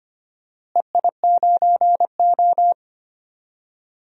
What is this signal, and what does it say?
Morse code, heard as a keyed tone, about 25 wpm, 700 Hz. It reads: EI9O
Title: DXCC-100 Call Signs - 4 Characters - Repeated 3x - 25wpm